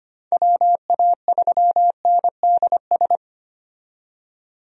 Morse code, heard as a keyed tone, about 25 wpm, 700 Hz. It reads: WA3NDS
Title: USA Call Signs - 6 Characters - 25wpm